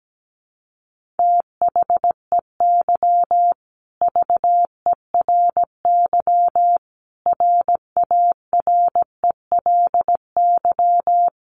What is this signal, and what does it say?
Morse code, heard as a keyed tone, about 17 wpm, 700 Hz. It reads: THEY VERY RARELY
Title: Sentences from Top 1400 Words 17wpm